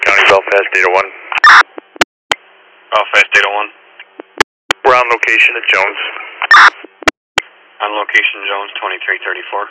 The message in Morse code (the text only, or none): none